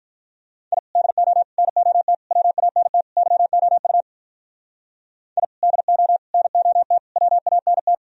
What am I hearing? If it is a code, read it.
Morse code, heard as a keyed tone, about 46 wpm, 700 Hz. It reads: I DO NOT WANT YOU I DO NOT WANT